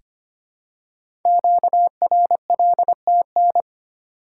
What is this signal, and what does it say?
Morse code, heard as a keyed tone, about 25 wpm, 700 Hz. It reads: QRLTN
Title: Code Groups - 5 Letters - 25wpm